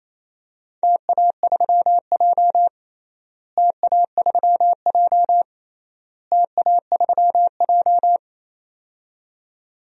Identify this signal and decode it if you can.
Morse code, heard as a keyed tone, about 28 wpm, 700 Hz. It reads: TA3J TA3J TA3J
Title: DXCC-100 Call Signs - 4 Characters - Repeated 3x - 28wpm